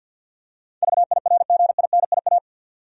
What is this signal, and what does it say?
Morse code, heard as a keyed tone, about 52 wpm, 700 Hz. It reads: VIRGINIA
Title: USA Common Names - 50wpm